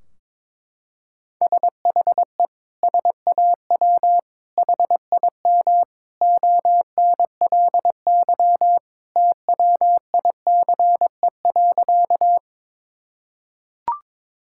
Morse code, heard as a keyed tone, about 22 wpm, 700 Hz. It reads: SHE SAW HIM ONLY TWICE.
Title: Sentences from Top 1600 Words 22wpm